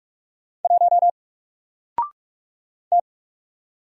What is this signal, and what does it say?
Morse code, heard as a keyed tone, about 48 wpm, 700 Hz. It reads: 1 T
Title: Single Character (Letter:Number:Punctuation:Prosign) Alt 45wpm